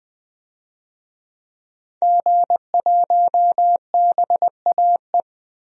Morse code, heard as a keyed tone, about 20 wpm, 700 Hz. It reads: G1BAE
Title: DXCC-100 Call Signs - 5 Characters - 20wpm